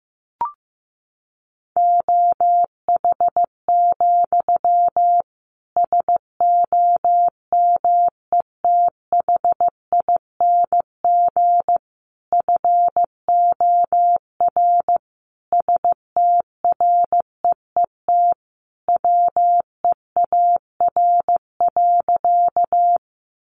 Morse code, heard as a keyed tone, about 15 wpm, 700 Hz. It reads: OH, SOMETHING FOR STREET WEAR.
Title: Sentences from Top 1500 Words 15wpm